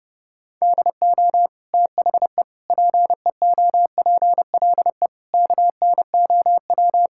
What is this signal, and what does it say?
Morse code, heard as a keyed tone, about 32 wpm, 700 Hz. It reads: DO THE PEOPLE KNOW
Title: Sentences from Top 1000 Words - Review Part 5 - 30wpm